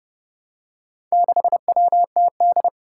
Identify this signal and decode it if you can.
Morse code, heard as a keyed tone, about 30 wpm, 700 Hz. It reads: 6WTD
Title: Four Challenging Letter-Number Combinations 30wpm